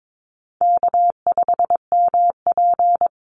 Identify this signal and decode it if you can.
Morse code, heard as a keyed tone, about 22 wpm, 700 Hz. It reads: K5MP
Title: Easy Call Signs Repeated 2x - 22wpm